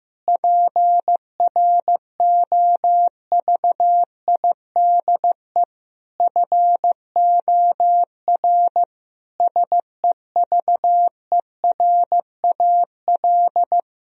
Morse code, heard as a keyed tone, about 16 wpm, 700 Hz. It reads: PROVIDE FOR SEVERAL
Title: Sentences from Top 1000 Words - Review Part 2 - 15wpm